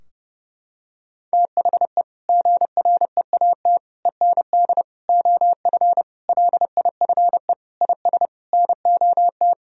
Morse code, heard as a keyed tone, about 30 wpm, 700 Hz. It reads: THE GREAT END OF LIFE IS NOT
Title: Sentences from Top 1000 Words - Review Part 4 - 30wpm